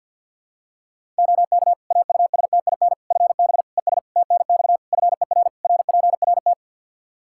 Morse code, heard as a keyed tone, about 47 wpm, 700 Hz. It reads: OK AUSTIN FB ES TNX FER RPRT
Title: Practice QSO - Exchange 2 - Info - 50wpm